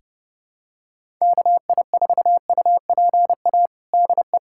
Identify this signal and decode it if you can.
Morse code, heard as a keyed tone, about 32 wpm, 700 Hz. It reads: KI4UPA DE
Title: USA Callsign DE Callsign 30wpm